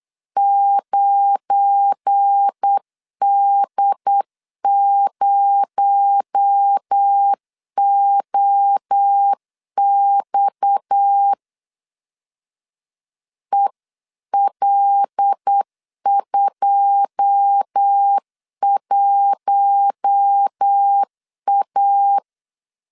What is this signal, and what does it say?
Morse code, heard as a keyed tone, about 8 wpm, 800 Hz. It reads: 9D0OX EL21A
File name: CW / DET / 1 9 1500C S B 8wpm 5624